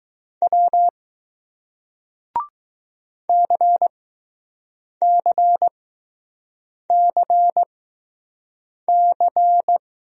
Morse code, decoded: W CCCC

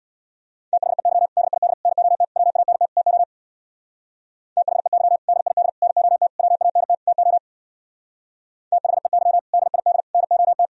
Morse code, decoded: THEY DID NOT WANT TO THEY DID NOT WANT TO THEY DID NOT

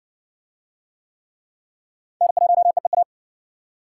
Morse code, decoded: N1EEA

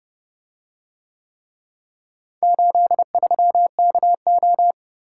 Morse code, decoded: 83KO

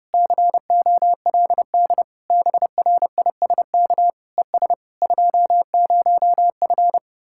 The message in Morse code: COLD BRISK ES 20F

